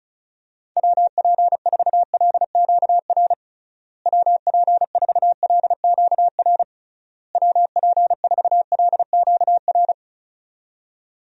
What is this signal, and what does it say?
Morse code, heard as a keyed tone, about 36 wpm, 700 Hz. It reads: WP4LQR WP4LQR WP4LQR
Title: USA Call Signs - 6 Characters - Repeated 3x - 35wpm